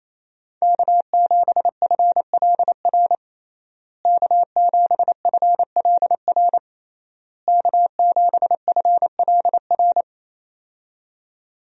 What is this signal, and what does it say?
Morse code, heard as a keyed tone, about 28 wpm, 700 Hz. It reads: K7FLR K7FLR K7FLR
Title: USA Call Signs - 5 Characters - Repeated 3x - 28wpm